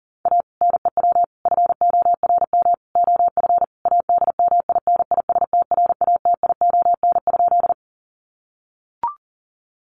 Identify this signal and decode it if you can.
Morse code, heard as a keyed tone, about 40 wpm, 700 Hz. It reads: A NEW FORM OF ADMINISTRATION?
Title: Sentences from Top 1300 Words 40wpm